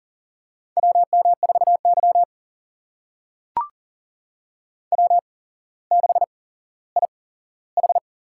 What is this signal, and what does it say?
Morse code, heard as a keyed tone, about 40 wpm, spaced as for 17 wpm, 700 Hz. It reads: WM4Y W6IH
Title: USA Call Signs - 8x Character Spacing - 40wpm